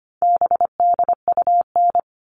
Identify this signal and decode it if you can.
Morse code, heard as a keyed tone, about 25 wpm, 700 Hz. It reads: BDUN
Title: All Cut Numbers - 4-Digit 25wpm